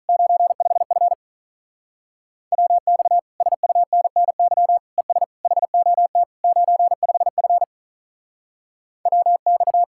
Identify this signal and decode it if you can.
Morse code, heard as a keyed tone, about 44 wpm, 700 Hz. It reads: 95F WX SUNNY ES HOT 95F WX
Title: Speed-Racing Common QSO Phrases - 35wpm